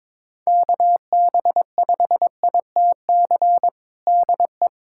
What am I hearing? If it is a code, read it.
Morse code, heard as a keyed tone, about 22 wpm, 700 Hz. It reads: KB5ITC DE
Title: USA Callsign DE Callsign - 6 Characters - 22wpm